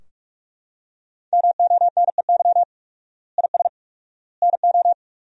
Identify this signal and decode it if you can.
Morse code, heard as a keyed tone, about 41 wpm, 700 Hz. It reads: MONEY IS NO
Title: 4x Word-Spacing - Sentences from Top 500 Words - 45wpm